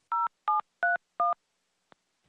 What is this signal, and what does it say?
DTMF, heard as telephone touch tones, tones of about 150 ms. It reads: 0731